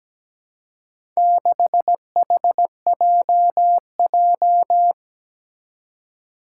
Morse code, heard as a keyed tone, about 17 wpm, 700 Hz. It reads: 6HJJ
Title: Four Challenging Letter-Number Combinations 17wpm